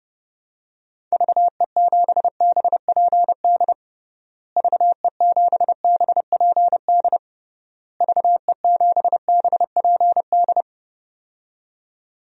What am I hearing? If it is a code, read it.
Morse code, heard as a keyed tone, about 30 wpm, 700 Hz. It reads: VE7BPD VE7BPD VE7BPD
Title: DXCC-100 Call Signs - 6 Characters - Repeated 3x - 30wpm